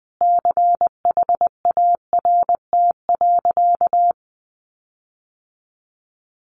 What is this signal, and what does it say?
Morse code, heard as a keyed tone, about 20 wpm, 700 Hz. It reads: CHART.
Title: Sentences from Top 1300 Words 20wpm